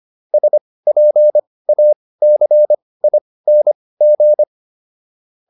Morse code, heard as a keyed tone, about 25 wpm, spaced as for 18 wpm, 600 Hz. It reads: SPACING